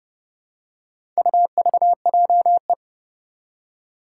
Morse code, heard as a keyed tone, about 31 wpm, 700 Hz. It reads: UVJE
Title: Four Challenging Letter-Number Combinations 30wpm